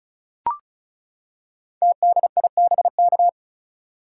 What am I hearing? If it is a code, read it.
Morse code, heard as a keyed tone, about 36 wpm, 700 Hz. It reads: TDIBK
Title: Code Groups - 5 Letters - 35wpm